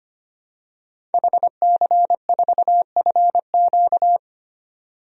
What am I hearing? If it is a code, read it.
Morse code, heard as a keyed tone, about 25 wpm, 700 Hz. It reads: HC4FQ
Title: DXCC-100 Call Signs - 5 Characters - 25wpm